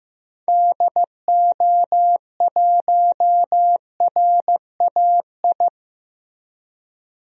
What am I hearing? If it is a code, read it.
Morse code, heard as a keyed tone, about 16 wpm, 700 Hz. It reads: DO1RAI
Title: DXCC-100 Call Signs - 6 Characters - Repeated 2x - 15wpm